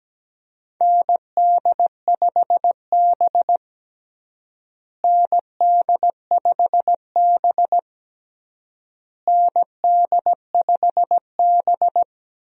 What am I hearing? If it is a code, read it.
Morse code, heard as a keyed tone, about 17 wpm, 700 Hz. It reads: ND5B ND5B ND5B
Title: USA Call Signs - 4 Characters - Repeated 3x - 17wpm